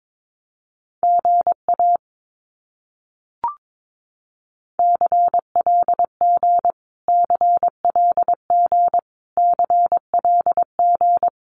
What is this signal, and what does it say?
Morse code, heard as a keyed tone, about 22 wpm, 700 Hz. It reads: GA CLG CLG CLG
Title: Abbreviations Repeated 3x - 22wpm